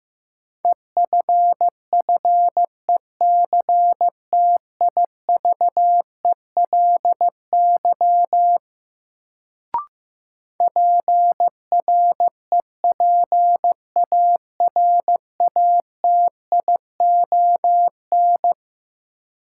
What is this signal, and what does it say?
Morse code, heard as a keyed tone, about 15 wpm, 700 Hz. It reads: EFFECTIVELY PREPARATION
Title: Sentences from Top 1400 Words - New Word Drill Alt - 15wpm